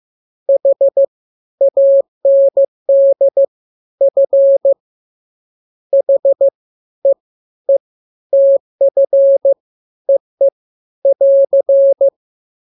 Morse code, heard as a keyed tone, about 15 wpm, 550 Hz. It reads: H AND F H E E TF EE <AR>
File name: jokes / joke 19 15 0000